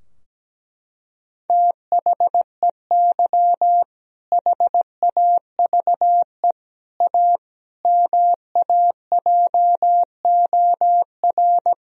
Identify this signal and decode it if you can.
Morse code, heard as a keyed tone, about 17 wpm, 700 Hz. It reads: THEY HAVE A MAJOR
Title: Sentences from Top 1000 Words - Review Part 4 - 17wpm